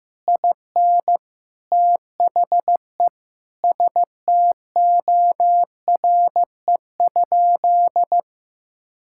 Morse code, heard as a keyed tone, about 15 wpm, 700 Hz. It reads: IN THE STORE?
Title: Sentences from Top 1000 Words - Review Part 6 - 15wpm